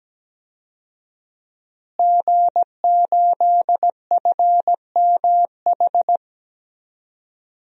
Morse code, heard as a keyed tone, about 17 wpm, 700 Hz. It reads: G8FMH
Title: DXCC-100 Call Signs - 5 Characters - Repeated 3x - 17wpm